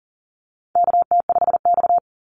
Morse code, heard as a keyed tone, about 40 wpm, 700 Hz. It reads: KT5X
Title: NA SOTA Chaser Callsigns 40wpm